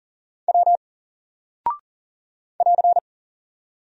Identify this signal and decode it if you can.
Morse code, heard as a keyed tone, about 40 wpm, 700 Hz. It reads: W <AR>
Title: Single Character (Letter:Number:Punctuation:Prosign) 40wpm